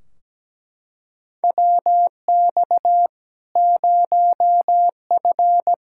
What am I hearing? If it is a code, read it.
Morse code, heard as a keyed tone, about 17 wpm, 700 Hz. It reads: WX 0F
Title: Practice QSO - Exchange 2 - Info - 17wpm